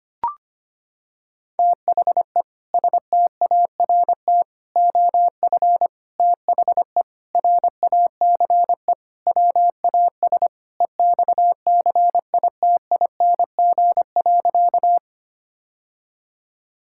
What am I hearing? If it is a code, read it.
Morse code, heard as a keyed tone, about 25 wpm, 700 Hz. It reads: THE START OF THE RACE WAS EXCITING.
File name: practice / 25wpm / Sentences with 4-Word Phrases - 25wpm